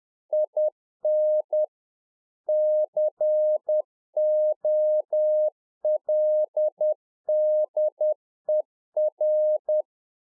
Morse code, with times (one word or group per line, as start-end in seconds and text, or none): 0.3-1.6 IN
2.5-9.8 COLDER